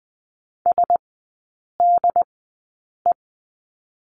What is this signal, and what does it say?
Morse code, heard as a keyed tone, about 20 wpm, spaced as for 8 wpm, 700 Hz. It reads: SDE
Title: 100 Lesson 17e - Letter u - Three Characters 20wpm